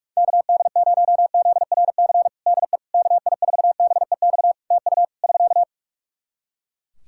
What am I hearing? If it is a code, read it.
Morse code, heard as a keyed tone, about 42 wpm, 700 Hz. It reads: KD0ZRC DE KI4BEX TU <SK>
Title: Practice QSO - Exchange 4 - Ending - 45wpm